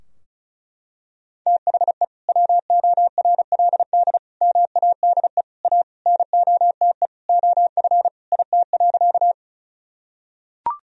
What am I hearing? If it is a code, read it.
Morse code, heard as a keyed tone, about 34 wpm, 700 Hz. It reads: THE WORLD MADE A NOTE OF IT.
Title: Sentences from Top 1000 Words - Review Part 2 - 35wpm